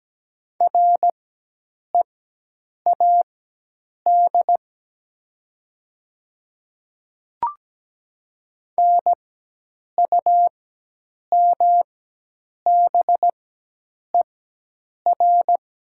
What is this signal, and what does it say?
Morse code, heard as a keyed tone, about 17 wpm, spaced as for 8 wpm, 700 Hz. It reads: READ NUMBER
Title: 4x Character Spacing - Top 300 Words - 17wpm